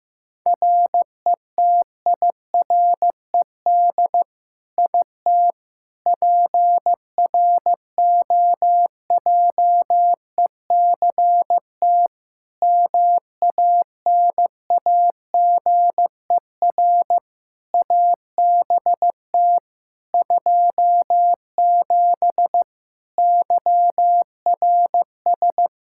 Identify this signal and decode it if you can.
Morse code, heard as a keyed tone, about 15 wpm, 700 Hz. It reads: RETIRED IT PROJECT MANAGER ABT 27 YRS
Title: Practice QSO - Exchange 3 - More Info - 15wpm